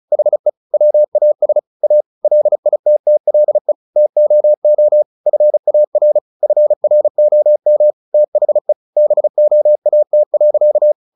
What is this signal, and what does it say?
Morse code, heard as a keyed tone, about 34 wpm, 600 Hz. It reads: HE WAS A LITTLE TOO FAR FROM THE BOAT.
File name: the old man and the sea 2 35wpm